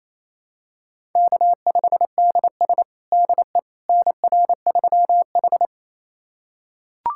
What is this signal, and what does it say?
Morse code, heard as a keyed tone, about 28 wpm, 700 Hz. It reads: K5DS DE NR3H
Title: USA Callsign DE Callsign - 4 Characters - 28wpm